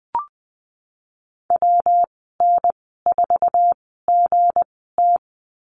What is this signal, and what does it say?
Morse code, heard as a keyed tone, about 20 wpm, spaced as for 15 wpm, 700 Hz. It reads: WN4GT
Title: 219 Lesson 35g - Number 0 - Call Signs 20wpm